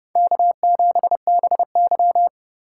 Morse code, heard as a keyed tone, about 30 wpm, 700 Hz. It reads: K7BY